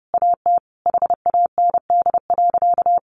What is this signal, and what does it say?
Morse code, heard as a keyed tone, about 30 wpm, 700 Hz. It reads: AT HAND.
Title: Sentences from Top 1300 Words 30wpm